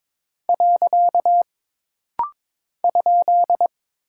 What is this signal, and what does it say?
Morse code, heard as a keyed tone, about 22 wpm, 700 Hz. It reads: . ?
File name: practice / 22wpm / Single Punctuation-Prosign - Rapid-Fire - 22wpm